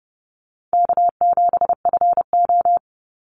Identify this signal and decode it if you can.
Morse code, heard as a keyed tone, about 30 wpm, 700 Hz. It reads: K7FO
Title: NA SOTA Chaser Callsigns 30wpm